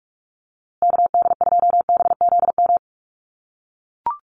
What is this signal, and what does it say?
Morse code, heard as a keyed tone, about 44 wpm, 700 Hz. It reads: KD2BZM
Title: USA Call Signs Repeated 2x 45wpm